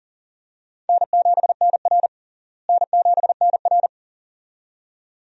Morse code, heard as a keyed tone, about 40 wpm, 700 Hz. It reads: N7NR N7NR
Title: USA Call Signs - 4 Characters - Repeated 2x - 40wpm